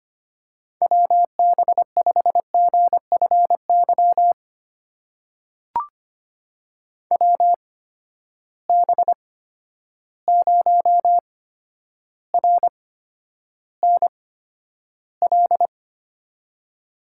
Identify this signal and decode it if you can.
Morse code, heard as a keyed tone, about 25 wpm, spaced as for 11 wpm, 700 Hz. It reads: WB5GFY WB0RNL
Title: USA Call Signs - 6 Characters - 8x Character Spacing - 25wpm